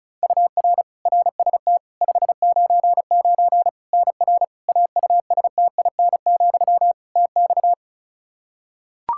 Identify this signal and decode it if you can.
Morse code, heard as a keyed tone, about 36 wpm, 700 Hz. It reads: UR RST 599 NR AUSTIN, TX
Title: Practice QSO - Exchange 1 - Beginning - 35wpm